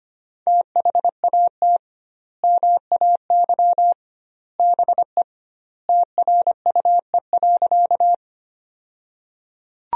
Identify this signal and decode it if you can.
Morse code, heard as a keyed tone, about 25 wpm, 700 Hz. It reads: THAT MAY BE TRUE.
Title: 2x Word-Spacing - Sentences from Top 400 Words - 25wpm